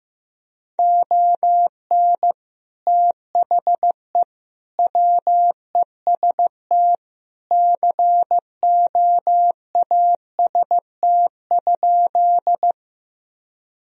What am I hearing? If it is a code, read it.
Morse code, heard as a keyed tone, about 15 wpm, 700 Hz. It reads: ON THE WEST COAST?
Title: Sentences from Top 1000 Words - Review Part 4 - 15wpm